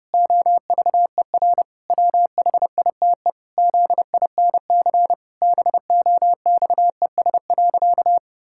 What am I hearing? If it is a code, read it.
Morse code, heard as a keyed tone, about 32 wpm, 700 Hz. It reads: OVER WHITE ZINC BOXES.
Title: Pangram Challenge v2 - 30wpm